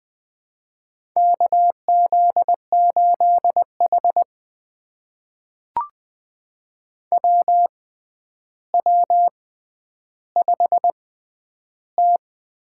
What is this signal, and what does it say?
Morse code, heard as a keyed tone, about 20 wpm, spaced as for 10 wpm, 700 Hz. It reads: KZ8H WW5T
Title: USA Call Signs - 4 Characters - 6x Character Spacing - 20wpm